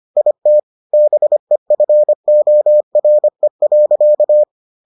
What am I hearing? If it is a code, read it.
Morse code, heard as a keyed tone, about 25 wpm, 600 Hz. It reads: IT BEFORE.